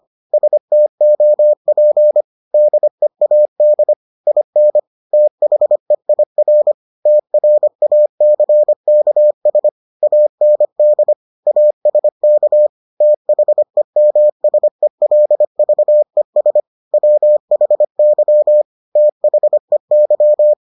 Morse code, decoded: STOP DEAD IN THEIR TRACKS AND ASK THEMSELVES WHY THEY

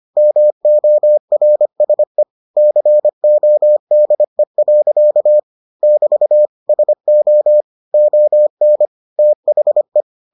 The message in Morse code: MORSE CODE. <BT> SO ON THE

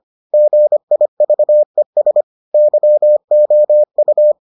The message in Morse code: GIVES YOU

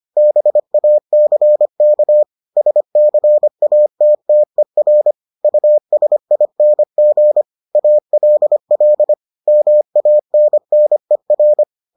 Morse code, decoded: BACK SCATTER USING ALL MANNER

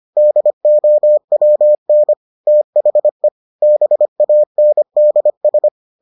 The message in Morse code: DOWN THE BANDS

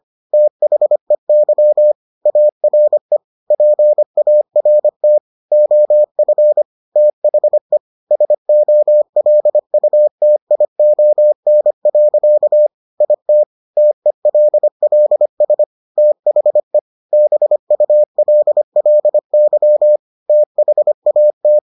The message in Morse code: THEY ARE PART OF THE SOLUTION. IT TELLS THE BULLY THAT

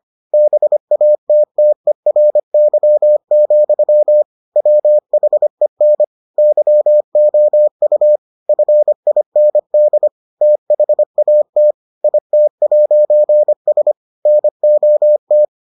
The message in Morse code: BATTERY, WHEN YOU FIND THAT IT'S NOT

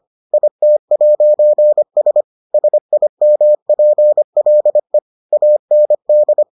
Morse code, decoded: IT'S SIMPLE AND

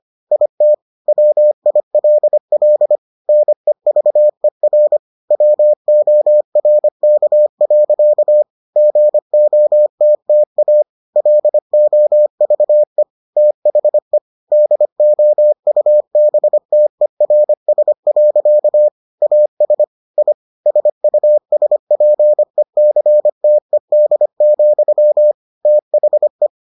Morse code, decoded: IT WILL NEVER WORK. GOTTA LOVE THE DOUBTERS. AS I SUSPECTED, THE

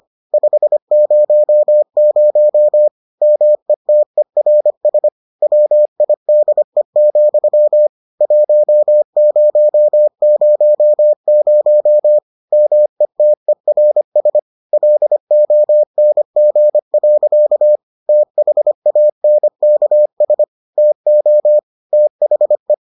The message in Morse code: 500 METERS WIDE, 1000 METERS LONG. THANKS TO THE

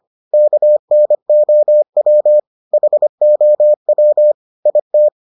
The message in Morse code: KNOW HOW IT

